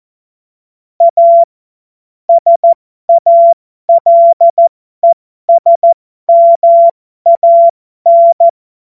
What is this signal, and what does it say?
Morse code, heard as a keyed tone, about 14 wpm, spaced as for 12 wpm, 680 Hz. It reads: A SALESMAN
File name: Morsekurs / Witze